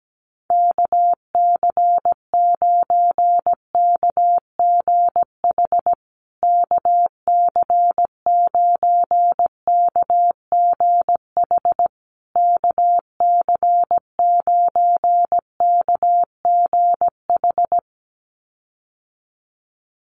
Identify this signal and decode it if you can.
Morse code, heard as a keyed tone, about 17 wpm, 700 Hz. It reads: KC9KGH KC9KGH KC9KGH
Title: USA Call Signs Repeated 3x 17wpm